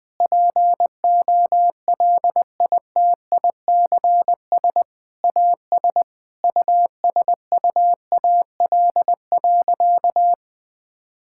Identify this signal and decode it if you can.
Morse code, heard as a keyed tone, about 20 wpm, 700 Hz. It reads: POLITICS AS USUAL.
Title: Sentences from Top 1000 Words - Review Part 6 - 20wpm